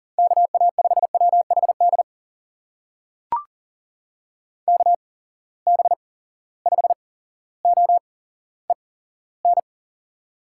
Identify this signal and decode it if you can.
Morse code, heard as a keyed tone, about 40 wpm, spaced as for 17 wpm, 700 Hz. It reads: KA5WHD KB5OEN
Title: USA Call Signs - 6 Characters - 8x Character Spacing - 40wpm